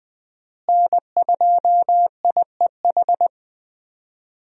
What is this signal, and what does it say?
Morse code, heard as a keyed tone, about 20 wpm, 700 Hz. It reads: N2IEH
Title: USA Call Signs - 5 Characters - Repeated 3x - 20wpm